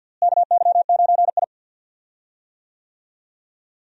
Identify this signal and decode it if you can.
Morse code, heard as a keyed tone, about 46 wpm, 700 Hz. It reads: KY9I
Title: USA Call Signs - 4 Characters - Repeated 3x - 50wpm